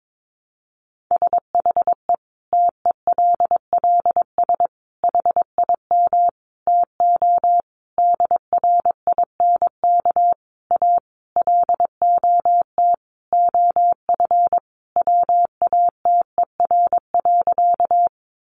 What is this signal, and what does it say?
Morse code, heard as a keyed tone, about 22 wpm, 700 Hz. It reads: SHE TELLS HIM TO DRINK A LOT OF WATER.
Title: Additional Easy Phrases 22wpm